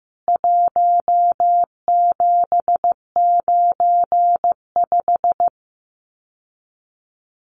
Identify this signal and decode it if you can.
Morse code, heard as a keyed tone, about 15 wpm, 700 Hz. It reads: 1795
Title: CWT Exchange 15wpm